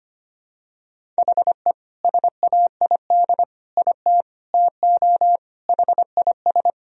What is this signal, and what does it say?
Morse code, heard as a keyed tone, about 25 wpm, 700 Hz. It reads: HE SAID IT TO HIS
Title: Sentences from Top 1000 Words - Review Part 1 - 25wpm